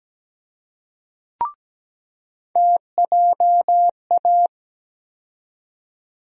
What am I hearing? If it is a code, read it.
Morse code, heard as a keyed tone, about 17 wpm, 700 Hz. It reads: TJA
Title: USA Call Sign Postfixes - 3 Letters 17wpm